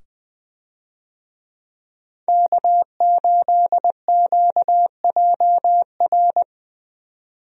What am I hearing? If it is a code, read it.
Morse code, heard as a keyed tone, about 20 wpm, 700 Hz. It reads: K8QJR
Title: USA Call Signs - 5 Characters - 20wpm